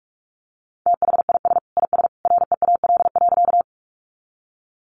Morse code, heard as a keyed tone, about 44 wpm, 700 Hz. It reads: THIS IS REAL.
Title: Sentences from Top 300 Words 45wpm